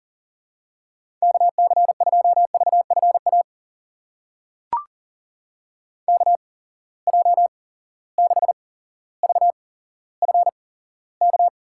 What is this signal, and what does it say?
Morse code, heard as a keyed tone, about 38 wpm, spaced as for 17 wpm, 700 Hz. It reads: KC2VFA KJ6VFK
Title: USA Call Signs - 6 Characters - 8x Character Spacing - 40wpm